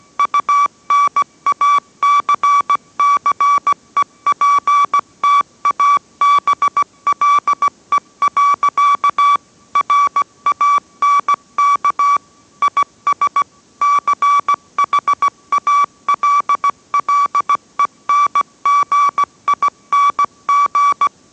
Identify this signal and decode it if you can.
Morse code, heard as a keyed tone, about 17 wpm, 1150 Hz. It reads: UNACCEPTABLE. RANK IS CHALLENGING